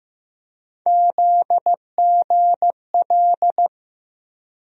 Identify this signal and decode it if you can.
Morse code, heard as a keyed tone, about 15 wpm, 700 Hz. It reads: ZGL
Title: USA Call Sign Postfixes - 3 Letters 15wpm